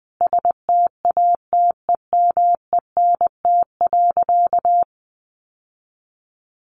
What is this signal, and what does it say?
Morse code, heard as a keyed tone, about 20 wpm, 700 Hz. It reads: STATEMENT.
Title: Sentences from Top 1200 Words 20wpm